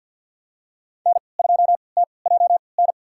Morse code, heard as a keyed tone, about 50 wpm, spaced as for 28 wpm, 690 Hz. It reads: N2TJN
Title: USA Call Signs - 5 Characters - 3x Character Spacing - 50wpm